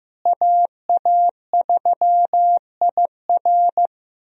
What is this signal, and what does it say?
Morse code, heard as a keyed tone, about 15 wpm, 700 Hz. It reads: AA3IR
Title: USA Call Signs - 5 Characters - Repeated 3x - 15wpm